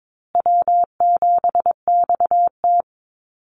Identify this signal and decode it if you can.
Morse code, heard as a keyed tone, about 22 wpm, 700 Hz. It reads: W7XT
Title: Easy Call Signs Repeated 2x - 22wpm